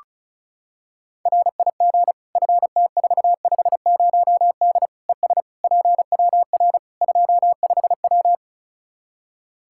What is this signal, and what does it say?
Morse code, heard as a keyed tone, about 36 wpm, 700 Hz. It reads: RIG FT450D ES PWR 25W
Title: Practice QSO - Exchange 2 - Info - 35wpm